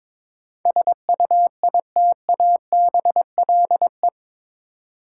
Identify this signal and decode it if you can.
Morse code, heard as a keyed tone, about 22 wpm, 700 Hz. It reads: SUITABLE
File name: practice / 22wpm / Eight Letter Words - 22wpm